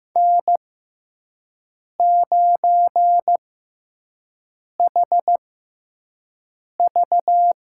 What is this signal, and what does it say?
Morse code, heard as a keyed tone, about 15 wpm, spaced as for 5 wpm, 700 Hz. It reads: N9HV
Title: USA Call Signs - 5 Characters - 6x Character Spacing - 15wpm